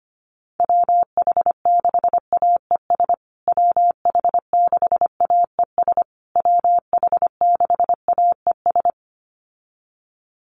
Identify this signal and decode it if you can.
Morse code, heard as a keyed tone, about 25 wpm, 700 Hz. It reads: WH6AES WH6AES WH6AES
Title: USA Call Signs Repeated 3x 25wpm